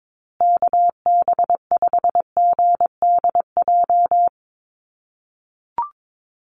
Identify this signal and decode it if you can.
Morse code, heard as a keyed tone, about 22 wpm, 700 Hz. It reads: KB5GDJ
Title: USA Call Signs Repeated 2x 22wpm